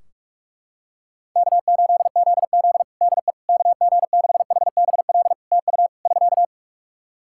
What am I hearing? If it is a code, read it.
Morse code, heard as a keyed tone, about 41 wpm, 700 Hz. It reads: K8ZZ DE KG6HBL TU <SK>
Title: Practice QSO - Exchange 4 - Ending - 45wpm